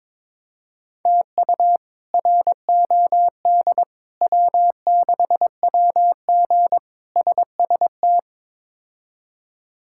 Morse code, heard as a keyed tone, about 21 wpm, 700 Hz. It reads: TU ROD W6WG SST
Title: K1USN SST QSO Practice - 22wpm